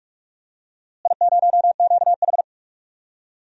Morse code, heard as a keyed tone, about 42 wpm, 700 Hz. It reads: I0QH
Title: DXCC-100 Call Signs - 4 Characters - Repeated 3x - 45wpm